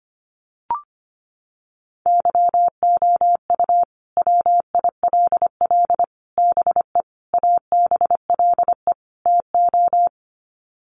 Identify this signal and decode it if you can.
Morse code, heard as a keyed tone, about 25 wpm, 700 Hz. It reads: YOU WILL BE ABLE TO
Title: Sets of 5 Words Encore2 25wpm